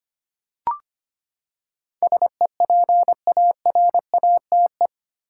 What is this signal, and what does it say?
Morse code, heard as a keyed tone, about 25 wpm, 700 Hz. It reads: SEPARATE
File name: Eight Letter Words - 25wpm